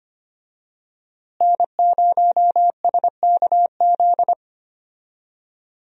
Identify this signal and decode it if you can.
Morse code, heard as a keyed tone, about 25 wpm, 700 Hz. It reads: N0SKZ
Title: USA Call Signs - 5 Characters - Repeated 3x - 25wpm